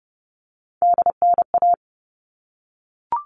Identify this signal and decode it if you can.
Morse code, heard as a keyed tone, about 30 wpm, 700 Hz. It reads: DNA
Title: All Cut Numbers - 3-Digit 30wpm